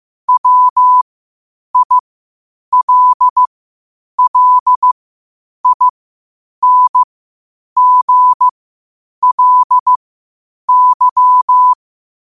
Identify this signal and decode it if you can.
Morse code, heard as a keyed tone, about 15 wpm, spaced as for 8 wpm, 1000 Hz. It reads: WILLINGLY
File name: qotdfiles / 8wpm